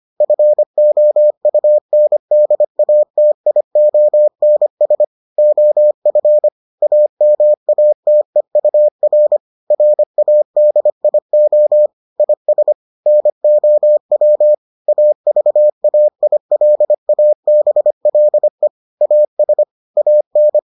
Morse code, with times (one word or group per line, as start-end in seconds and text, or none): 0.2-5.0 FOUNDATIONS
5.4-6.5 OF
6.8-9.4 AMATEUR
9.7-11.9 RADIO
12.2-12.7 IS
13.1-14.5 NOW
14.9-18.7 AVAILABLE
19.0-19.6 AS
20.0-20.6 AN